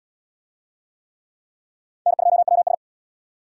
Morse code, heard as a keyed tone, about 75 wpm, 700 Hz. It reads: N2WS